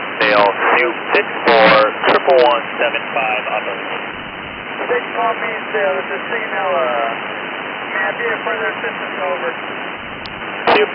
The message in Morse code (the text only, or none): none